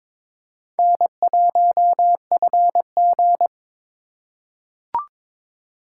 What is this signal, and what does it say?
Morse code, heard as a keyed tone, about 22 wpm, 700 Hz. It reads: N1FG
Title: USA Call Signs - 4 Characters - 2x Character Spacing - 22wpm